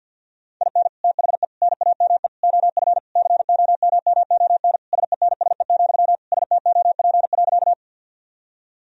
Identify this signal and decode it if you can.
Morse code, heard as a keyed tone, about 50 wpm, 700 Hz. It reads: IN THE NAME OF COMMON SENSE, STOP.
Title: Sentences from Top 1000 Words - Review Part 4 - 50wpm